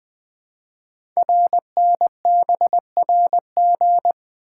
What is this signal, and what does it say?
Morse code, heard as a keyed tone, about 20 wpm, 700 Hz. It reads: RNBRG
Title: Code Groups - 5 Letters - 20wpm